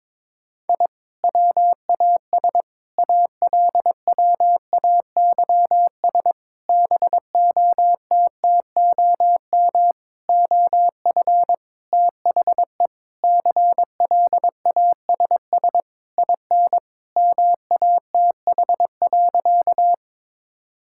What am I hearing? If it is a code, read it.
Morse code, heard as a keyed tone, about 23 wpm, 700 Hz. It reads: I WAS ALWAYS BOTTOM OF THE CLASS IN MATH.
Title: Sentences from Top 1000 Words - Review Part 6 - 22wpm